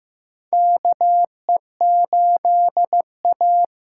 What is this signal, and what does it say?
Morse code, heard as a keyed tone, about 15 wpm, 700 Hz. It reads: KE8A
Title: USA Call Signs - 6 Characters - 6x Character Spacing - 15wpm